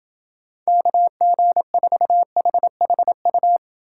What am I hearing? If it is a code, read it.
Morse code, heard as a keyed tone, about 28 wpm, 700 Hz. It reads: KG4HHU